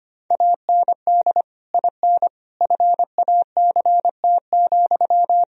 Morse code, heard as a keyed tone, about 25 wpm, 700 Hz. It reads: AND IN FACT,
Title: Sentences from Top 1000 Words - Review Part 3 - 25wpm